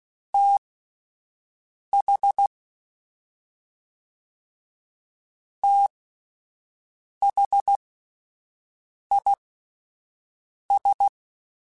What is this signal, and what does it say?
Morse code, heard as a keyed tone, about 16 wpm, spaced as for 5 wpm, 800 Hz. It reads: TH THIS
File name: Combinations / Combinations-mp3s / Combinations@5